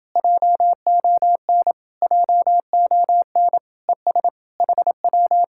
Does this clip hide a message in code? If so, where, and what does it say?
Yes: Morse code from 0.2 to 5.4 s, heard as a keyed tone, about 30 wpm, 700 Hz. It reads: JON JON ES HW